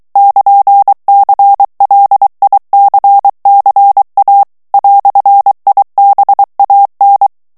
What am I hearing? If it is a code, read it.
Morse code, heard as a keyed tone, about 23 wpm, 800 Hz. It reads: <KN>CLICCA "IBAN
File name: morse mp3 / ITALIEN texte clair 1 vitesse 22 mots minute